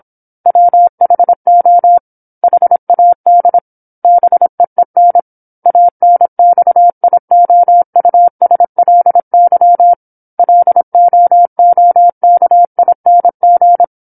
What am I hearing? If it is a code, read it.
Morse code, heard as a keyed tone, about 27 wpm, 700 Hz. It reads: WHO HAD BEEN ANXIOUSLY LOOKING
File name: 08-Chapter-08 26 WPM